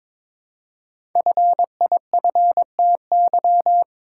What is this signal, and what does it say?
Morse code, heard as a keyed tone, about 22 wpm, 700 Hz. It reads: FIFTY
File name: 5 Letter Words - First 3 Letters Given - 22wpm